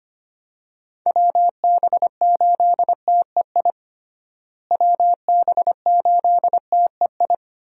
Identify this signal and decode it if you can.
Morse code, heard as a keyed tone, about 25 wpm, 700 Hz. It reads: WB8TEI WB8TEI
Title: USA Call Signs - 6 Characters - Repeated 2x - 25wpm